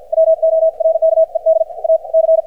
Morse code, heard as a keyed tone, about 56 wpm, spaced as for 48 wpm, 590 Hz. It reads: WORKED A C
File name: archived / CRSnet